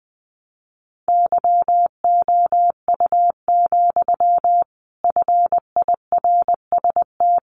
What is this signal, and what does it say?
Morse code, heard as a keyed tone, about 20 wpm, 700 Hz. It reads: YOU, FIRST
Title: A Sound Of Thunder 20wpm